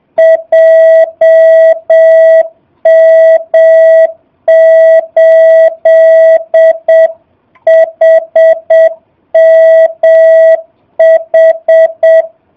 Morse code, heard as a keyed tone, about 7 wpm, 650 Hz. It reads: JM8HMH
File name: jh8cbh / 2019cw / SP05- w